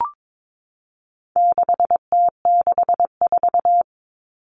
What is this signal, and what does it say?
Morse code, heard as a keyed tone, about 22 wpm, 700 Hz. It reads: 6T64
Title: All Cut Numbers - 4-Digit 22wpm